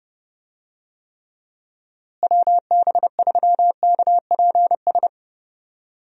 Morse code, heard as a keyed tone, about 29 wpm, 700 Hz. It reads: WB3KPS
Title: USA Call Signs - 6 Characters - Repeated 2x - 30wpm